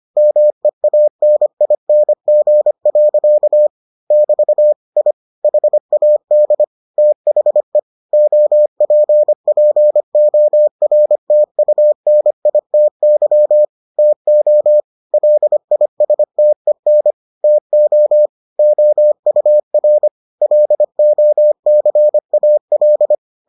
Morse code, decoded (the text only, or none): MEANING. <BT> I HAD THE OPPORTUNITY TO LISTEN TO OUR LOCAL